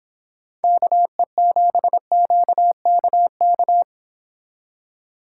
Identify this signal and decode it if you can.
Morse code, heard as a keyed tone, about 25 wpm, 700 Hz. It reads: KE7QKK